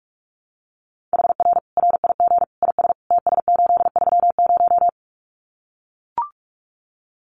Morse code, heard as a keyed tone, about 44 wpm, 700 Hz. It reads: HR RIG IS TS830
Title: Common QSO Phrases 45wpm